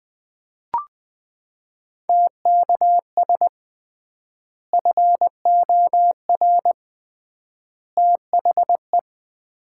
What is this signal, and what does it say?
Morse code, heard as a keyed tone, about 20 wpm, 700 Hz. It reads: TKS FOR THE
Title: Common QSO Phrases - 3x Word-Spacing - 20wpm